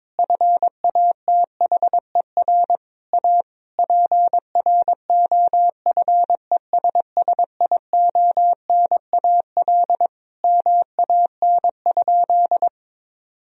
Morse code, heard as a keyed tone, about 22 wpm, 700 Hz. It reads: FATHER A PROFESSIONAL MAN?
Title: Sentences from Top 1000 Words - Review Part 4 - 22wpm